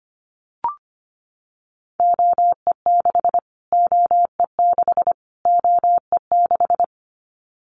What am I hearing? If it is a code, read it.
Morse code, heard as a keyed tone, about 25 wpm, 700 Hz. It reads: OE6 OE6 OE6
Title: Callsign Prefixes for Top 25 DXCC - Repeated 3x - 25wpm